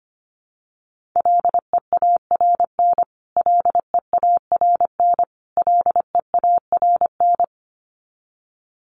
Morse code, heard as a keyed tone, about 25 wpm, 700 Hz. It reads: LEARN LEARN LEARN
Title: Top 300 Words Repeated 25wpm